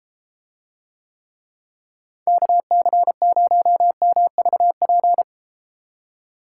Morse code, heard as a keyed tone, about 34 wpm, 700 Hz. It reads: KC0MVP